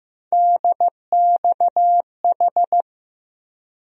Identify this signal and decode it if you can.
Morse code, heard as a keyed tone, about 15 wpm, 700 Hz. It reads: DXH
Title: USA Call Sign Postfixes - 3 Letters 15wpm